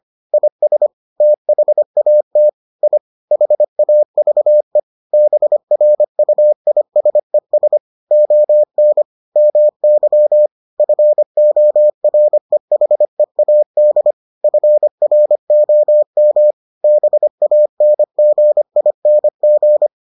Morse code, heard as a keyed tone, about 25 wpm, 600 Hz. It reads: IS THAT I HAVE BRUISES ON MY FOREHEAD FROM BANGING